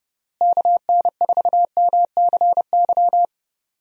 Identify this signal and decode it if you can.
Morse code, heard as a keyed tone, about 31 wpm, 700 Hz. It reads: KN4MCY
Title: USA Call Signs - 6 Characters - Repeated 2x - 30wpm